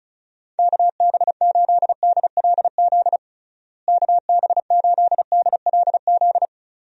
Morse code, heard as a keyed tone, about 36 wpm, 700 Hz. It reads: KB8DLZ KB8DLZ
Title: USA Call Signs - 6 Characters - Repeated 3x - 35wpm